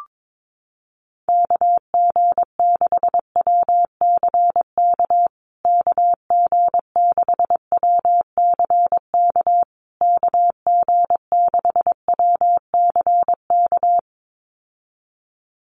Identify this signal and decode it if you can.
Morse code, heard as a keyed tone, about 22 wpm, 700 Hz. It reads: KG6WCK KG6WCK KG6WCK
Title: USA Call Signs Repeated 3x 22wpm